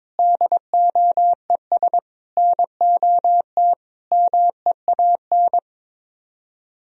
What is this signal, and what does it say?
Morse code, heard as a keyed tone, about 22 wpm, 700 Hz. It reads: DOES NOT MEAN
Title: Speed-Racing - Sets of 5 Words 22wpm